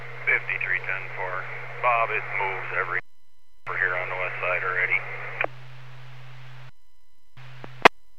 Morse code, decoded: none

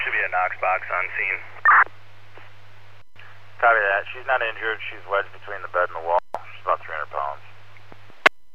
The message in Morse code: none

